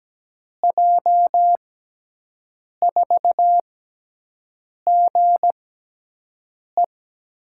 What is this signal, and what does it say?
Morse code, heard as a keyed tone, about 17 wpm, spaced as for 6 wpm, 700 Hz. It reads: J4GE
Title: USA Call Signs - 5 Characters - 6x Character Spacing - 17wpm